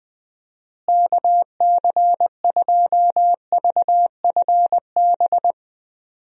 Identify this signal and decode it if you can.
Morse code, heard as a keyed tone, about 20 wpm, 700 Hz. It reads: KC2VFB
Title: USA Call Signs - 6 Characters - Repeated 2x - 20wpm